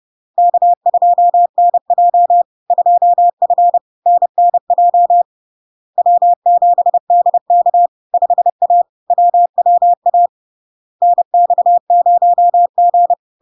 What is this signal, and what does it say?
Morse code, heard as a keyed tone, about 30 wpm, 700 Hz. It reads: K2NJ 2F NNJ W7DK 5A WWA NX0G
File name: LARK-FD-2022-Log-in-CW-30wpm-700-Hz